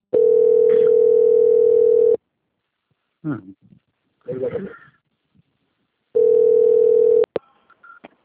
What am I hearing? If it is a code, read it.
Ringback tone (North America), 440 + 480 Hz, with speech.
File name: out- - - - - .27838